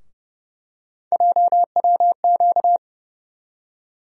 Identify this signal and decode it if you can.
Morse code, heard as a keyed tone, about 32 wpm, 700 Hz. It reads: JWQ
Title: USA Call Sign Postfixes - 3 Letters 30wpm